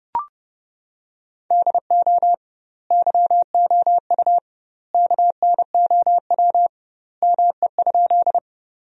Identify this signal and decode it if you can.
Morse code, heard as a keyed tone, about 30 wpm, 700 Hz. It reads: DO YOU KNOW ME?
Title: 2x Word-Spacing - Sentences from Top 200 Words - 30wpm